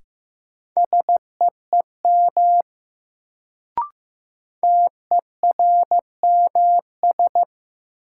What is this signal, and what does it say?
Morse code, heard as a keyed tone, about 15 wpm, 700 Hz. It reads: SEEM TERMS
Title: Sets of 3 Words - New Word Drill - 15wpm